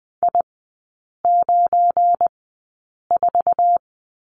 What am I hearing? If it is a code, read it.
Morse code, heard as a keyed tone, about 20 wpm, spaced as for 8 wpm, 700 Hz. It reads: I94
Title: 172 Lesson 28e - Number 9 - Three Characters 20wpm